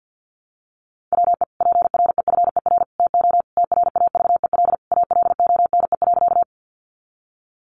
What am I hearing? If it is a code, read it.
Morse code, heard as a keyed tone, about 50 wpm, 700 Hz. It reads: WE PREFER TO TRAVEL ALONE.
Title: Sentences from Top 1100 Words 50wpm